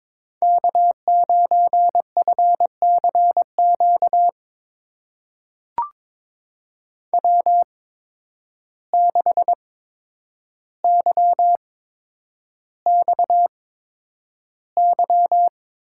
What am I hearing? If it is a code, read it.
Morse code, heard as a keyed tone, about 22 wpm, spaced as for 9 wpm, 700 Hz. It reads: K9FCQ W6YXY
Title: USA Call Signs - 5 Characters - 8x Character Spacing - 22wpm